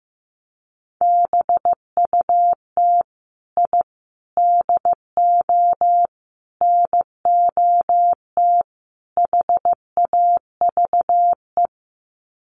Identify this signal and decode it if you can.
Morse code, heard as a keyed tone, about 15 wpm, 700 Hz. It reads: BUT I DO NOT HAVE
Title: Sets of 5 Words Encore 15wpm